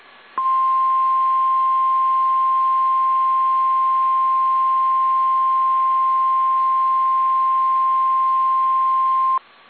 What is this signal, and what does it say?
Emergency Alert System (SAME) tones (the 1050 Hz attention tone). It is heard from 0.4 to 9.4 s.
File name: eas / KSDZKDJL / KSDZKDJL